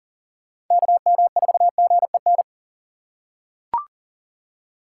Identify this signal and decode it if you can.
Morse code, heard as a keyed tone, about 40 wpm, 700 Hz. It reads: KM4GEN
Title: USA Call Signs - 6 Characters - 4x Character Spacing - 40wpm